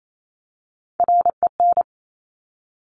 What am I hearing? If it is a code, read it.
Morse code, heard as a keyed tone, about 28 wpm, 700 Hz. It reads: REN
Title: Top Three Letter Combinations in Words 28wpm